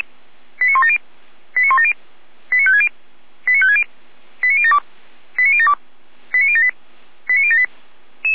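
Five-tone selective call (ZVEI), heard as quick runs of five tones, tones of about 70 ms. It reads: 89170 89170 89560 89560 89072 89072 89078 89078
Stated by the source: ZVEI five-tone selective call received on an SDR.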